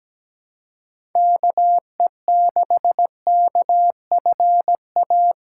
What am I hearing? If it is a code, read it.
Morse code, heard as a keyed tone, about 17 wpm, 700 Hz. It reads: KE6KFA
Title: USA Callsign DE Callsign - 6 Characters - 17wpm